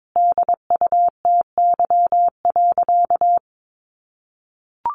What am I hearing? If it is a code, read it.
Morse code, heard as a keyed tone, about 22 wpm, 700 Hz. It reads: DUTY.
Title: Sentences from Top 1500 Words 22wpm